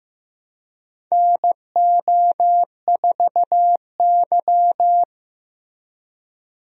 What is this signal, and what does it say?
Morse code, heard as a keyed tone, about 15 wpm, 700 Hz. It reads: NO4Y
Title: USA Call Signs - 4 Characters - Repeated 2x - 15wpm